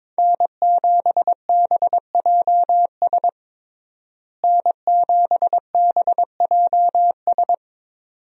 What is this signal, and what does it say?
Morse code, heard as a keyed tone, about 22 wpm, 700 Hz. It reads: N7BJS N7BJS
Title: USA Call Signs - 5 Characters - Repeated 2x - 22wpm